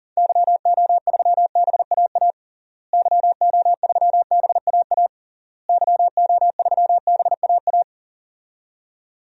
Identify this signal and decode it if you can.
Morse code, heard as a keyed tone, about 40 wpm, 700 Hz. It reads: YO3BAA YO3BAA YO3BAA
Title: DXCC-100 Call Signs - 6 Characters - Repeated 3x - 40wpm